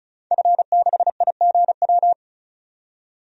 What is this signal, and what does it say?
Morse code, heard as a keyed tone, about 36 wpm, 700 Hz. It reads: F6IGW
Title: DXCC-100 Call Signs - 5 Characters - 35wpm